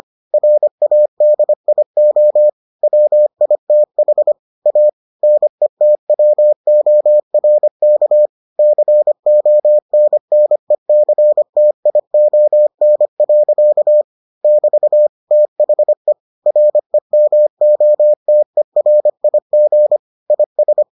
Morse code, heard as a keyed tone, about 25 wpm, 600 Hz. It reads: RADIO WITH A NETWORK CONNECTION. <BT> THE REMOTERIG IS